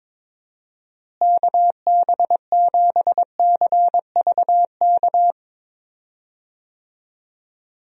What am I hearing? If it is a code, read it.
Morse code, heard as a keyed tone, about 22 wpm, 700 Hz. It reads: KB7CVK
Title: USA Call Signs - 6 Characters - Repeated 2x - 22wpm